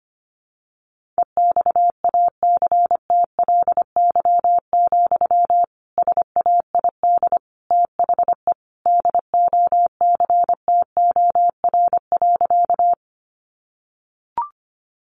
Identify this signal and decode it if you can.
Morse code, heard as a keyed tone, about 25 wpm, 700 Hz. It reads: EXACTLY, SAID THE DOCTOR.
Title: Sentences from Top 1500 Words 25wpm